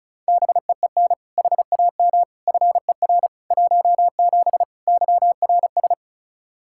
Morse code, heard as a keyed tone, about 36 wpm, 700 Hz. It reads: BEEN HAM FER 17 YRS
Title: K1USN SST QSO Practice 35wpm